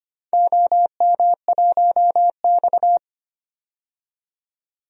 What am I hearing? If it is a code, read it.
Morse code, heard as a keyed tone, about 25 wpm, 700 Hz. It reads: OM1X
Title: DXCC-100 Call Signs - 4 Characters - 25wpm